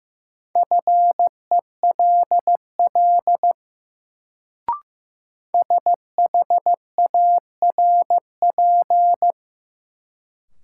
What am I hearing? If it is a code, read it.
Morse code, heard as a keyed tone, about 15 wpm, 700 Hz. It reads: FELL SHARP
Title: Sentences from Top 1600 Words - New Word Drill - 15wpm